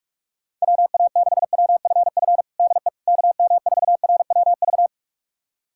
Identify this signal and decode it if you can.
Morse code, heard as a keyed tone, about 42 wpm, 700 Hz. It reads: WA6WFF DE KM4RWV
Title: USA Callsign DE Callsign 45wpm